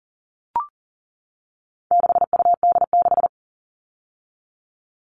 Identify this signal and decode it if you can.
Morse code, heard as a keyed tone, about 40 wpm, 700 Hz. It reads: 6UD6
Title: All Cut Numbers - 4-Digit 40wpm